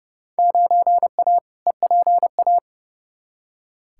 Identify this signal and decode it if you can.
Morse code, heard as a keyed tone, about 30 wpm, 700 Hz. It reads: 9A EPA
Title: Field Day Exchanges Encore - 30wpm